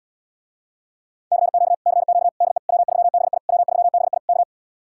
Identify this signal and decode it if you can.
Morse code, heard as a keyed tone, about 79 wpm, 700 Hz. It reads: CQ CQ DE W3BI W3BI K